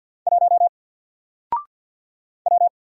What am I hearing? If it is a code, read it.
Morse code, heard as a keyed tone, about 50 wpm, 700 Hz. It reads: 1 W
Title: Single Letter-Number Alt 50wpm